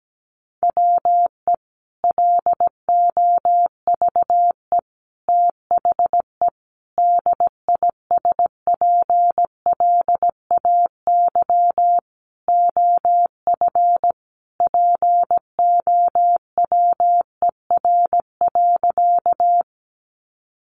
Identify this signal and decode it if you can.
Morse code, heard as a keyed tone, about 17 wpm, 700 Hz. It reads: WE LOVE THE DISPLAY OF POWER.
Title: Sentences from Top 1200 Words 17wpm